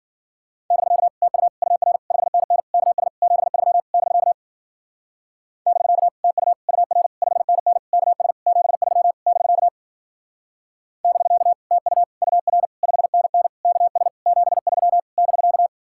Morse code, heard as a keyed tone, about 48 wpm, 700 Hz. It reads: <BK> TU UR 5NN KS 73 <BK> <BK> TU UR 5NN KS 73 <BK> <BK> TU UR 5NN KS 73 <BK>
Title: Speed-Racing Common QSO Phrases - 40wpm